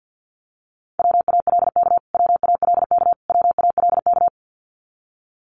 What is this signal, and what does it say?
Morse code, heard as a keyed tone, about 50 wpm, 700 Hz. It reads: WALK WALK WALK
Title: Top 500 Words Repeated 50wpm